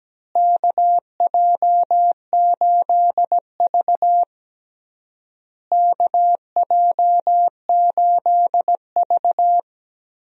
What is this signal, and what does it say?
Morse code, heard as a keyed tone, about 17 wpm, 700 Hz. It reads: KJ8V KJ8V
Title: USA Call Signs - 4 Characters - Repeated 3x - 17wpm